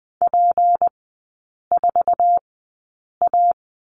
Morse code, heard as a keyed tone, about 20 wpm, spaced as for 8 wpm, 700 Hz. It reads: P4A
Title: 152 Lesson 25e - Letter p - Three Characters 20wpm